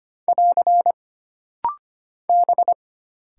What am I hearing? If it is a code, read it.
Morse code, heard as a keyed tone, about 25 wpm, 700 Hz. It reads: <AR> B
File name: Single Character (Letter:Number:Punctuation:Prosign) - Rapid-Fire - 25wpm